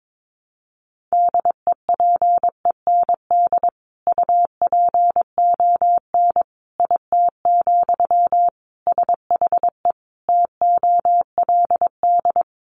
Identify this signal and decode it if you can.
Morse code, heard as a keyed tone, about 22 wpm, 700 Hz. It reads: DEPEND UPON IT, SHE TOLD
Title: Sentences from Top 1300 Words 22wpm